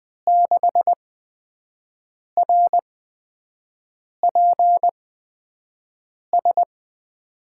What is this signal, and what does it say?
Morse code, heard as a keyed tone, about 21 wpm, spaced as for 6 wpm, 700 Hz. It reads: 6RPS
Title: USA Call Signs - 6 Characters - 8x Character Spacing - 20wpm